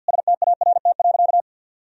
Morse code, heard as a keyed tone, about 51 wpm, 700 Hz. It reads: START.
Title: Sentences from Top 1000 Words - Review Part 1 - 50wpm